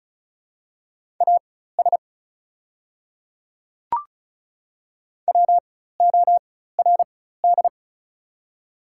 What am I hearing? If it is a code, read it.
Morse code, heard as a keyed tone, about 36 wpm, spaced as for 16 wpm, 690 Hz. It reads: AS WORD
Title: 4x Character Spacing - Top 100 Words - 35wpm